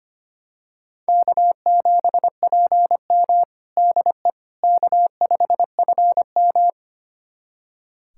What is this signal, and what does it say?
Morse code, heard as a keyed tone, about 25 wpm, 700 Hz. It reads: K7PM DE K5FM
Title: USA Callsign DE Callsign - 4 Characters - 25wpm